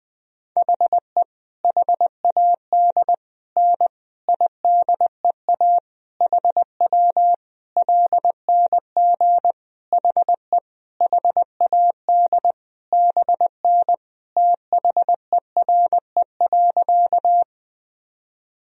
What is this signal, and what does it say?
Morse code, heard as a keyed tone, about 20 wpm, 700 Hz. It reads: HE HAD N IDEA HW LNG HE HAD BN THERE.
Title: Sentences from Top 300 Words and Abbreviations 20wpm